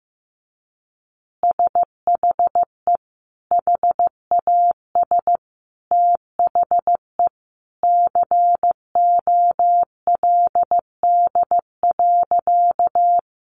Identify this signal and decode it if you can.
Morse code, heard as a keyed tone, about 15 wpm, 700 Hz. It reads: SHE HAS THE COLD.
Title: Easy Phrases 15wpm